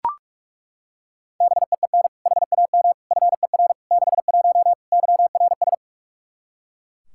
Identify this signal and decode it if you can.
Morse code, heard as a keyed tone, about 41 wpm, 700 Hz. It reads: BEEN HAM FER 61 YRS
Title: K1USN SST QSO Practice 45wpm